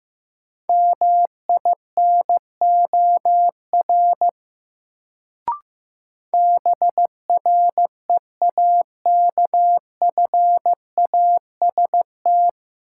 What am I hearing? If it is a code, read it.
Morse code, heard as a keyed tone, about 15 wpm, 700 Hz. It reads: MINOR BREAKFAST
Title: Sentences from Top 1500 Words - New Word Drill - 15wpm